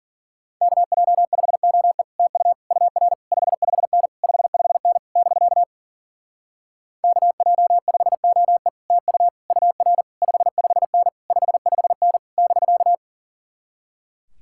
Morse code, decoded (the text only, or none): KJ5OE TU UR 55N 55N <BK> KJ5OE TU UR 55N 55N <BK>